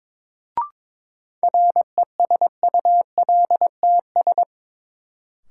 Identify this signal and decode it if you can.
Morse code, heard as a keyed tone, about 21 wpm, 700 Hz. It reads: RESULTS
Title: Sets of 4 Words - New Word Drill - 22wpm